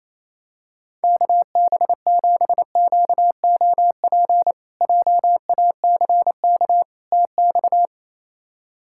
Morse code, heard as a keyed tone, about 28 wpm, 700 Hz. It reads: KB7QOP JACK TX
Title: K1USN SST QSO Practice - 28wpm